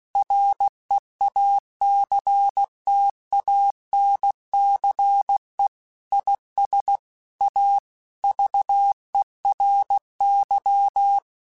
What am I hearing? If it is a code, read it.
Morse code, heard as a keyed tone, about 16 wpm, 800 Hz. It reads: REACTANCE IS A VERY
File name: Technical Subjects / Technical Subjects-mp3s / Reactance @15